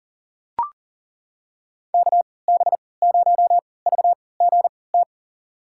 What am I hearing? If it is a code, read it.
Morse code, heard as a keyed tone, about 40 wpm, spaced as for 23 wpm, 700 Hz. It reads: KB0VGT
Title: USA Call Signs - 3x Character Spacing - 40wpm